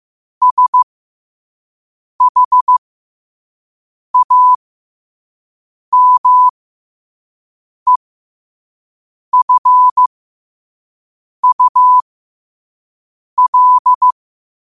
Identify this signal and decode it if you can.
Morse code, heard as a keyed tone, about 15 wpm, spaced as for 5 wpm, 1000 Hz. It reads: SHAMEFUL
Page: New quotes every day in morse code at 5 Words per minute.